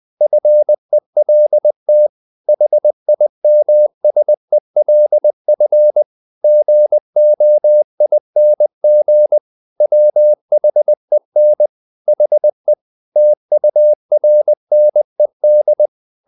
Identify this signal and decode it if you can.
Morse code, heard as a keyed tone, about 20 wpm, 600 Hz. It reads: FELT HIMSELF GOING WHEN HE TURNED
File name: the old man and the sea part 2 20wpm